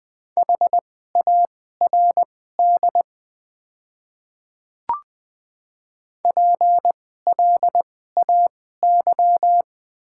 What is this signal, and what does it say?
Morse code, heard as a keyed tone, about 20 wpm, spaced as for 15 wpm, 700 Hz. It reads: HARD PLAY
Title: 2x Character Spacing - Top 300 Words - 20wpm